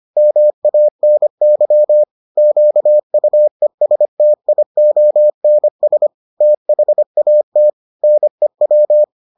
Morse code, heard as a keyed tone, about 25 wpm, 600 Hz. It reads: MANY QUESTIONS THAT NEW